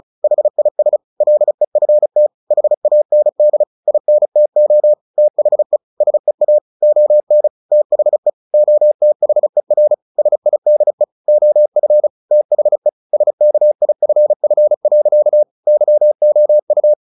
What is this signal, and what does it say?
Morse code, heard as a keyed tone, about 34 wpm, 600 Hz. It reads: HIS LEFT HAND INTO THE SEA ON THE OTHER SIDE OF THE SKIFF. YOU
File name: the old man and the sea 2 35wpm